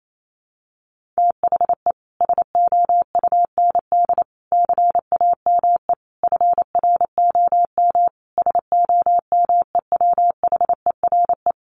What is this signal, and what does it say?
Morse code, heard as a keyed tone, about 28 wpm, 700 Hz. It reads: THE SOUND CAME FROM SOMEWHERE
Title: Sentences from Top 1400 Words 28wpm